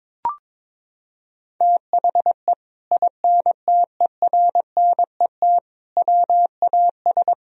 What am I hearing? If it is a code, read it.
Morse code, heard as a keyed tone, about 22 wpm, 700 Hz. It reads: THE INTERNET WAS
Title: Sentences from Top 1000 Words - Review Part 4 - 22wpm